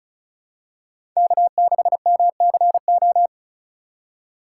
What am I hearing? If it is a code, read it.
Morse code, heard as a keyed tone, about 34 wpm, 700 Hz. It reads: K6MCO
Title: USA Call Signs - 5 Characters - 4x Character Spacing - 35wpm